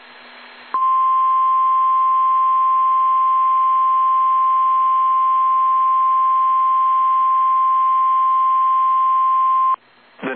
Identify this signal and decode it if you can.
Emergency Alert System (SAME) tones (the 1050 Hz attention tone). It is heard from 0.7 to 9.8 s.
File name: eas / KSDZKDJL / KSDZKDJL